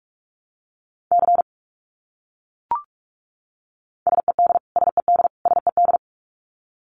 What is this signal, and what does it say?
Morse code, heard as a keyed tone, about 44 wpm, 700 Hz. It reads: C SED SED SED
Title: Abbreviations Repeated 3x - 45wpm